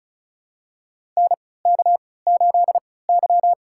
Morse code, heard as a keyed tone, about 36 wpm, spaced as for 20 wpm, 700 Hz. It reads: NK8Y
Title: USA Call Signs - 4 Characters - 3x Character Spacing - 35wpm